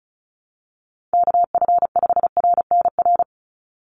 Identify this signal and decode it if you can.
Morse code, heard as a keyed tone, about 34 wpm, 700 Hz. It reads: KF5RNR
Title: USA Call Signs 35wpm